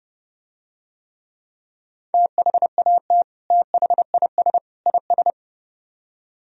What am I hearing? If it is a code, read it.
Morse code, heard as a keyed tone, about 30 wpm, 700 Hz. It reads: THAT THIS IS